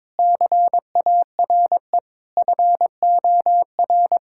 Morse code, heard as a keyed tone, about 22 wpm, 700 Hz. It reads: CARE FOR
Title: Sentences from Top 1000 Words - Review Part 6 - 22wpm